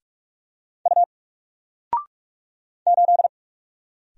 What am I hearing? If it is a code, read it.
Morse code, heard as a keyed tone, about 44 wpm, 700 Hz. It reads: U 8
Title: Single Letter-Number Alt 45wpm